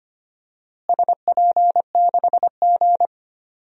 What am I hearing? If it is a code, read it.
Morse code, heard as a keyed tone, about 25 wpm, 700 Hz. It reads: SP6G
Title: DXCC-100 Call Signs - 4 Characters - Repeated 3x - 25wpm